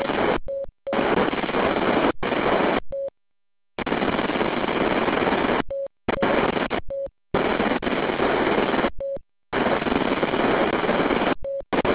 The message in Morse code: none